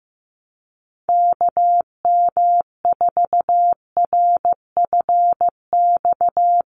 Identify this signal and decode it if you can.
Morse code, heard as a keyed tone, about 15 wpm, 700 Hz. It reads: KM4RFX
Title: USA Call Signs 15wpm